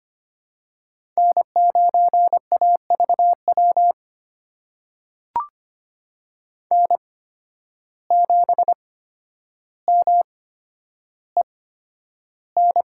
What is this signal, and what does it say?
Morse code, heard as a keyed tone, about 25 wpm, spaced as for 11 wpm, 700 Hz. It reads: N9AVW N7MEN
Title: USA Call Signs - 5 Characters - 8x Character Spacing - 25wpm